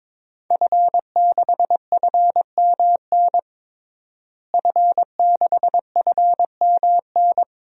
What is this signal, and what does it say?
Morse code, heard as a keyed tone, about 22 wpm, 700 Hz. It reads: F6FMN F6FMN
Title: DXCC-100 Call Signs - 5 Characters - Repeated 3x - 22wpm